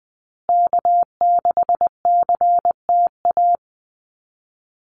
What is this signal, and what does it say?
Morse code, heard as a keyed tone, about 20 wpm, 700 Hz. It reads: K6CTA
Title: Call Signs Repeated 2x - 20wpm